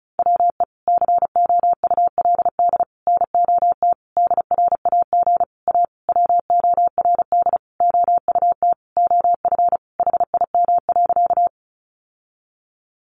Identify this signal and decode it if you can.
Morse code, heard as a keyed tone, about 34 wpm, 700 Hz. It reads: WE COULD NOT DRAG A WORD OUT OF HIM.
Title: Sentences from Top 1500 Words 35wpm